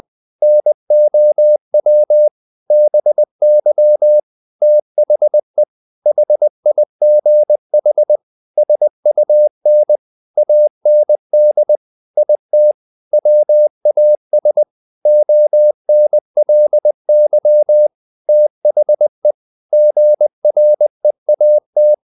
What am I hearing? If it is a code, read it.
Morse code, heard as a keyed tone, about 20 wpm, 600 Hz. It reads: NOW BY THE HIGH SUN AND IT WAS ONLY THE GREAT
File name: the old man and the sea / the old man and the sea part 1 20wpm